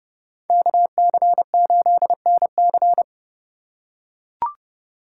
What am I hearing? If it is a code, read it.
Morse code, heard as a keyed tone, about 30 wpm, 700 Hz. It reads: KC8NC